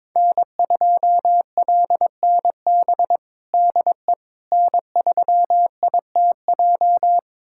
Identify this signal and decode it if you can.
Morse code, heard as a keyed tone, about 22 wpm, 700 Hz. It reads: N2LNB DE N3ITJ
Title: USA Callsign DE Callsign - 5 Characters - 22wpm